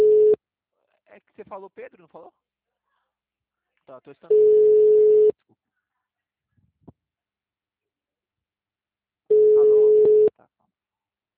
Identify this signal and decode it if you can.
Ringback tone (Europe, 425 Hz), with speech.